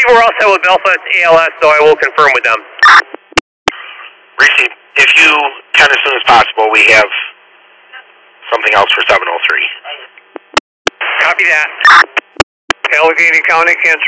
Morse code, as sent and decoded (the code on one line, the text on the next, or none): none